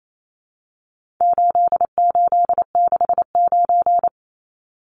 Morse code, heard as a keyed tone, about 28 wpm, 700 Hz. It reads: 8869
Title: Numbers - 4-Digit 28wpm